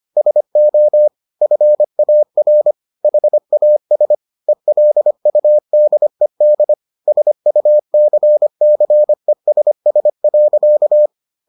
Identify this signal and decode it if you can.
Morse code, heard as a keyed tone, about 25 wpm, 600 Hz. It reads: SO FAR HAS ELUDED SUCCESS.